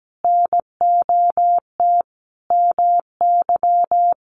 Morse code, heard as a keyed tone, about 17 wpm, 700 Hz. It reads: NOT MY
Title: Sentences from Top 1100 Words 17wpm